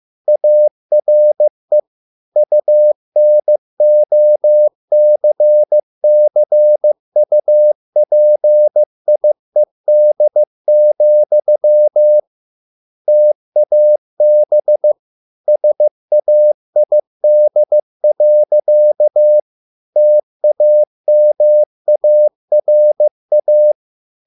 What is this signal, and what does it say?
Morse code, heard as a keyed tone, about 15 wpm, 600 Hz. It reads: ARE UNOCCUPIED, TAB SAID. TAMARA